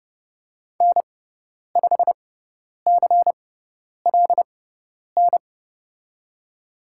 Morse code, heard as a keyed tone, about 30 wpm, spaced as for 10 wpm, 700 Hz. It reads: N5CLN
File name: Lesson 16g - Letter c - Call Signs 30wpm